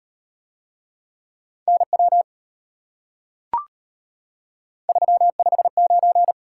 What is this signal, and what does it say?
Morse code, heard as a keyed tone, about 32 wpm, 700 Hz. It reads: NW 359